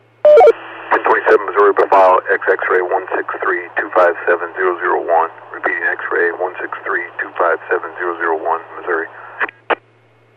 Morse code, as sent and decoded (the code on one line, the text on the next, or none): none